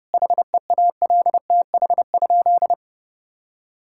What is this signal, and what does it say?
Morse code, heard as a keyed tone, about 32 wpm, 700 Hz. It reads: HEALTH?
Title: Sentences from Top 1000 Words - Review Part 5 - 30wpm